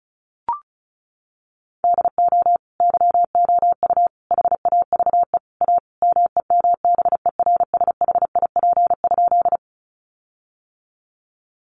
Morse code, heard as a keyed tone, about 34 wpm, 700 Hz. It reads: DO YOU HAVE A MEMBERSHIP?
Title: Sentences from Top 1400 Words 35wpm